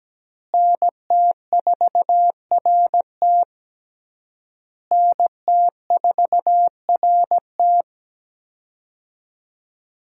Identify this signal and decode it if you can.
Morse code, heard as a keyed tone, about 17 wpm, 700 Hz. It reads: NT4RT NT4RT
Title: USA Call Signs - 5 Characters - Repeated 3x - 17wpm